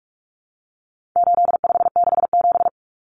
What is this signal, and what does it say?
Morse code, heard as a keyed tone, about 44 wpm, 700 Hz. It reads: 8567
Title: Numbers - 4-Digit 45wpm